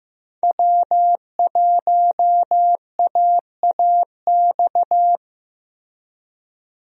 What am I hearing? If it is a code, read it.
Morse code, heard as a keyed tone, about 15 wpm, 700 Hz. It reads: W1AAX
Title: USA Call Signs - 5 Characters - Repeated 2x - 15wpm